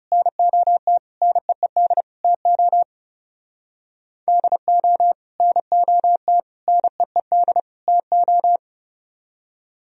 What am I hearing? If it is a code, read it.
Morse code, heard as a keyed tone, about 37 wpm, 700 Hz. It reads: NOT NEED TO DO NOT NEED TO
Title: Speed-Racing - Sets of 4 Words 30wpm